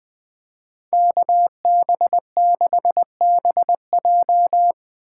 Morse code, heard as a keyed tone, about 20 wpm, 700 Hz. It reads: KB6BJ
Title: USA Call Signs - 5 Characters - 2x Character Spacing - 20wpm